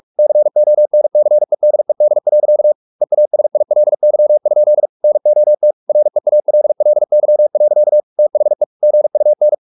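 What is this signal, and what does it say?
Morse code, heard as a keyed tone, about 44 wpm, 600 Hz. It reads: CONCEDED. EASILY? NOT REALLY. THE GUN
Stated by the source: Morse practice at 45 wpm, the novel Never by Ken Follett in English, from CW med Gnister.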